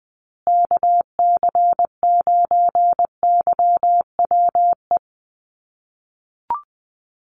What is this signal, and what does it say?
Morse code, heard as a keyed tone, about 20 wpm, 700 Hz. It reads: KC9YWE
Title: USA Call Signs Repeated 3x 20wpm